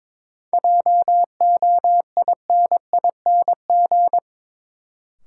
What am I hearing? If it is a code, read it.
Morse code, heard as a keyed tone, about 22 wpm, 700 Hz. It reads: JOINING
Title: Sets of 5 Words - New Word Drill - 22wpm